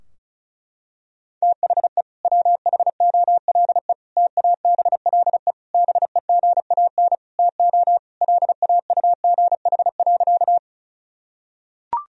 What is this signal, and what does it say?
Morse code, heard as a keyed tone, about 34 wpm, 700 Hz. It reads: THE WHOLE TABLE BEGAN TO LAUGH.
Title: Sentences from Top 1000 Words - Review Part 2 - 35wpm